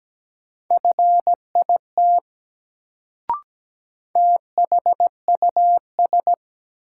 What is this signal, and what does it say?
Morse code, heard as a keyed tone, about 17 wpm, 700 Hz. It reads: FIT THUS
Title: Sentences from Top 800 Words - New Word Drill - 17wpm